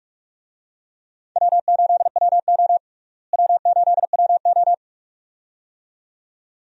Morse code, heard as a keyed tone, about 46 wpm, 700 Hz. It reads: W8WO W8WO
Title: USA Call Signs - 4 Characters - Repeated 2x - 45wpm